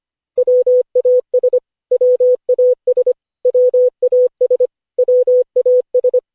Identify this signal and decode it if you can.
Morse code, heard as a keyed tone, about 25 wpm, 500 Hz. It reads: WAS WAS WAS WAS
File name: cw / Session5 cw training